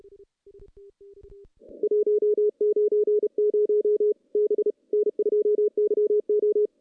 Morse code, heard as a keyed tone, about 31 wpm, 410 Hz. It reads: 190 B N2YO
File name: contests / CW-SS-k5tr- -0000